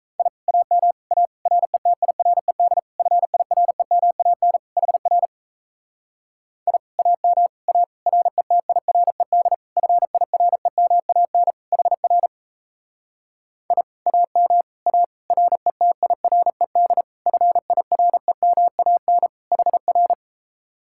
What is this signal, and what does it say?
Morse code, heard as a keyed tone, about 37 wpm, 700 Hz. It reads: I AM A RETIRED FIREMAN HR I AM A RETIRED FIREMAN HR I AM A RETIRED FIREMAN HR